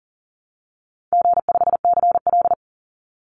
Morse code, heard as a keyed tone, about 40 wpm, 700 Hz. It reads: G5CL
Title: Call Signs Repeated 2x - 40wpm